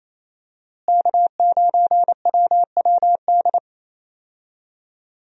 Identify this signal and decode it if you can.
Morse code, heard as a keyed tone, about 28 wpm, 700 Hz. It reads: K9WWD
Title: USA Call Signs - 5 Characters - Repeated 3x - 28wpm